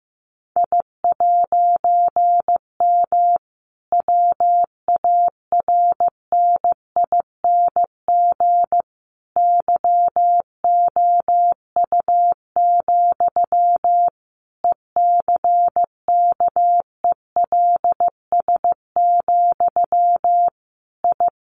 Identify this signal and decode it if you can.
Morse code, heard as a keyed tone, about 15 wpm, 700 Hz. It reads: I'M WARNING YOU, ECKELS, I
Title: A Sound Of Thunder 15wpm